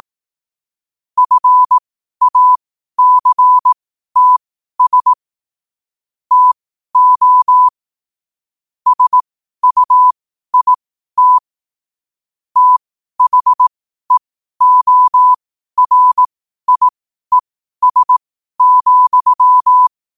Morse code, heard as a keyed tone, about 18 wpm, spaced as for 12 wpm, 1000 Hz. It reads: FACTS TO SUIT THEORIES,
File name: qotdfiles / 12wpm